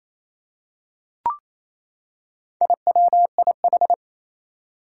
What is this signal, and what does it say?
Morse code, heard as a keyed tone, about 27 wpm, 700 Hz. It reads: IWIH
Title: Four Challenging Letter-Number Combinations 28wpm